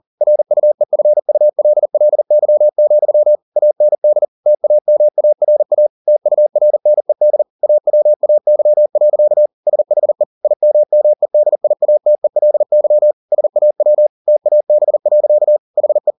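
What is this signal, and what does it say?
Morse code, heard as a keyed tone, about 40 wpm, 600 Hz. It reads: RUEFULLY, AND TAMARA TURNED AWAY. SHE IMMEDIATELY SAW TAB. HE
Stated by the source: Morse code at 40 wpm, practice text, CW med Gnister.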